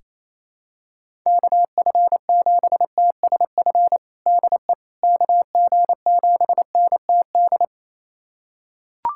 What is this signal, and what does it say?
Morse code, heard as a keyed tone, about 28 wpm, 700 Hz. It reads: KF7TSF DE KG7NTD
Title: USA Callsign DE Callsign - 6 Characters - 28wpm